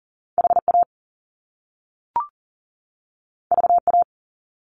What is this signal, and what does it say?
Morse code, heard as a keyed tone, about 40 wpm, 700 Hz. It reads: HA VA
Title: Top Two Letter Combinations in Words 40wpm